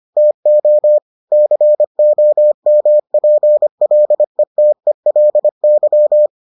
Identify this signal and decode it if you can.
Morse code, heard as a keyed tone, about 25 wpm, 600 Hz. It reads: TO COMPLETELY